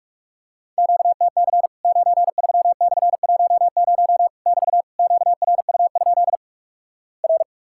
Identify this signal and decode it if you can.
Morse code, heard as a keyed tone, about 46 wpm, 700 Hz. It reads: QTC 93/10 <BT> QRU? R
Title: Worked All Europe DX Contest - QTC Exchange - 45wpm